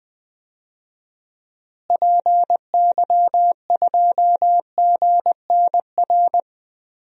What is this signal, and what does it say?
Morse code, heard as a keyed tone, about 20 wpm, 700 Hz. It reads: PY2GNR
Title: DXCC-100 Call Signs - 6 Characters - Repeated 2x - 20wpm